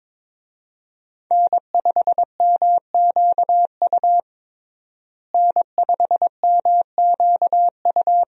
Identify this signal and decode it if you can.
Morse code, heard as a keyed tone, about 22 wpm, 700 Hz. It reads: N5MQU N5MQU
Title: USA Call Signs - 5 Characters - Repeated 2x - 22wpm